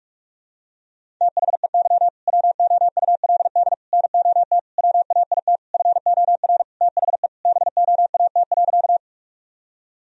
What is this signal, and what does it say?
Morse code, heard as a keyed tone, about 46 wpm, 700 Hz. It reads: THEY WOULD NOT WAIT FOR THE BOAT.
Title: Sentences from Top 1000 Words - Review Part 3 - 45wpm